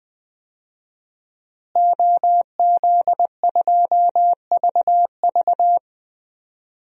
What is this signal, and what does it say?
Morse code, heard as a keyed tone, about 20 wpm, 700 Hz. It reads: OZ2VV
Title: DXCC-100 Call Signs - 5 Characters - Repeated 2x - 20wpm